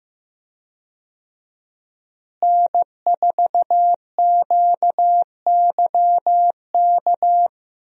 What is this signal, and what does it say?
Morse code, heard as a keyed tone, about 15 wpm, 700 Hz. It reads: N4QYK
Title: USA Call Signs - 5 Characters - Repeated 2x - 15wpm